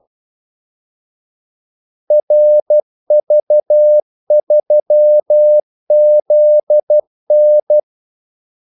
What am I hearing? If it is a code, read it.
Morse code, heard as a keyed tone, about 12 wpm, 600 Hz. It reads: RV3ZN
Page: Kaldesignaler 12 wpm